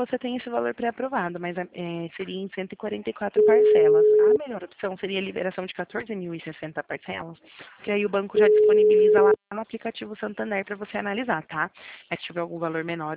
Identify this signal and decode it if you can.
Ringback tone (Europe, 425 Hz), with speech.
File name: out- -2017- - - .21196